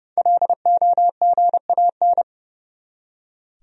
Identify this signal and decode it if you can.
Morse code, heard as a keyed tone, about 32 wpm, 700 Hz. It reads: LOGAN
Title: USA Common Names - 30wpm